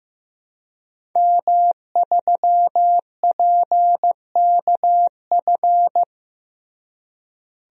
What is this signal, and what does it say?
Morse code, heard as a keyed tone, about 15 wpm, 700 Hz. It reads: M3PKF
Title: DXCC-100 Call Signs - 5 Characters - Repeated 3x - 15wpm